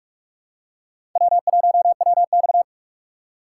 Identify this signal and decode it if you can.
Morse code, heard as a keyed tone, about 42 wpm, 700 Hz. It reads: W1WX
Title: USA Call Signs - 4 Characters - 6x Character Spacing - 45wpm